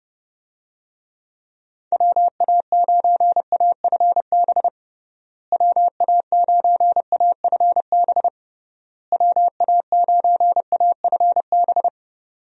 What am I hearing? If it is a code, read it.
Morse code, heard as a keyed tone, about 32 wpm, 700 Hz. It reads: WA9AFB WA9AFB WA9AFB
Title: USA Call Signs - 6 Characters - Repeated 3x - 30wpm